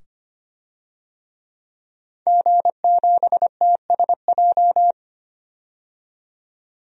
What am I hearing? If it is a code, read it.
Morse code, heard as a keyed tone, about 25 wpm, 700 Hz. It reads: G7TSJ
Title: DXCC-100 Call Signs - 5 Characters - 25wpm